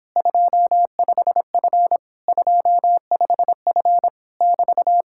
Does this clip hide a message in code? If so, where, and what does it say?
Yes: Morse code from 0.2 to 5.0 s, heard as a keyed tone, about 26 wpm, 700 Hz. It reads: 25F 25F <BT>